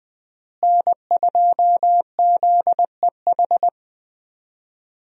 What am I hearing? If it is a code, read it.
Morse code, heard as a keyed tone, about 21 wpm, 700 Hz. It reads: N2ZEH
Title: USA Call Signs - 5 Characters - Repeated 2x - 20wpm